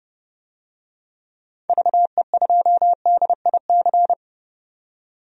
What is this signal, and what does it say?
Morse code, heard as a keyed tone, about 32 wpm, 700 Hz. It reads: VE2DIC
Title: DXCC-100 Call Signs - 6 Characters - 30wpm